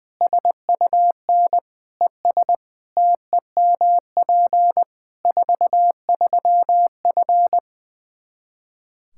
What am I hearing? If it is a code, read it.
Morse code, heard as a keyed tone, about 21 wpm, 700 Hz. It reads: SUN ES TEMP 43F
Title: Practice QSO - Exchange 2 - Info - 20wpm